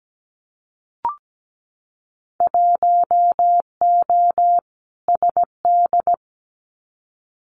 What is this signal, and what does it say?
Morse code, heard as a keyed tone, about 17 wpm, 700 Hz. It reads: 1O SD
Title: Winter Field Day Exchanges Encore 17wpm